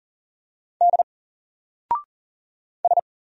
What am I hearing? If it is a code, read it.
Morse code, heard as a keyed tone, about 39 wpm, 700 Hz. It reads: D S
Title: Single Character (Letter:Number:Punctuation:Prosign) 40wpm